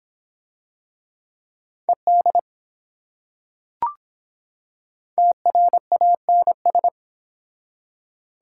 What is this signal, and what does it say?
Morse code, heard as a keyed tone, about 24 wpm, 700 Hz. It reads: ED TRANS